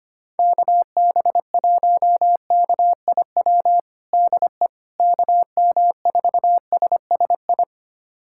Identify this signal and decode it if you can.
Morse code, heard as a keyed tone, about 25 wpm, 700 Hz. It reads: KB1KIW DE KM4SSI
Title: USA Callsign DE Callsign 25wpm